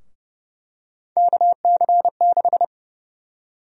Morse code, heard as a keyed tone, about 30 wpm, 700 Hz. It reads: KC6
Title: USA Call Sign Prefixes - 30wpm